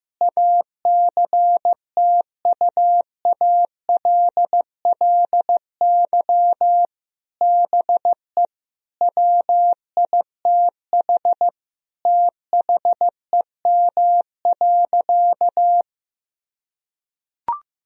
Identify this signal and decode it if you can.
Morse code, heard as a keyed tone, about 16 wpm, 700 Hz. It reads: ACTUALLY BE WITH THEM.
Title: Sentences from Top 1000 Words - Review Part 4 - 15wpm